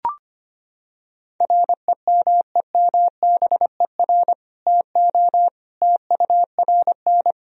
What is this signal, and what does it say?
Morse code, heard as a keyed tone, about 25 wpm, 700 Hz. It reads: REMEMBER TO TURN
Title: Sentences from Top 1000 Words - Review Part 3 - 25wpm